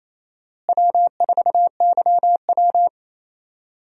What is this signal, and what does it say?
Morse code, heard as a keyed tone, about 28 wpm, 700 Hz. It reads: W4YW
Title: USA Call Signs - 4 Characters - Repeated 2x - 28wpm